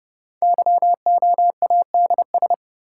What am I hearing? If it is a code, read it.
Morse code, heard as a keyed tone, about 30 wpm, 700 Hz. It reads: YOADS
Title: Code Groups - 5 Letter:Numbers - 30wpm